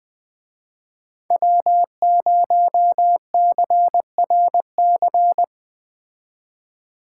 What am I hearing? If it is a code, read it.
Morse code, heard as a keyed tone, about 20 wpm, 700 Hz. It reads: W0CRC
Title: Speed Racing - USA 1x3 Call Signs 17wpm